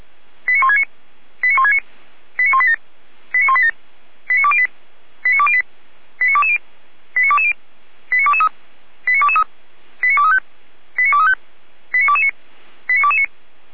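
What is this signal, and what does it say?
Five-tone selective call (ZVEI), heard as quick runs of five tones, tones of about 70 ms. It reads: 89169 89169 89187 89187 89208 89208 89220 89220 89223 89223 89235 89235 89229 89229
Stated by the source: ZVEI five-tone call 89174 received by SDR.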